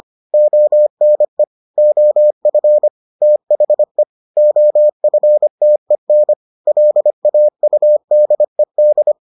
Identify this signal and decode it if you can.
Morse code, heard as a keyed tone, about 25 wpm, 600 Hz. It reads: ONE OF THE OFTEN LAUDED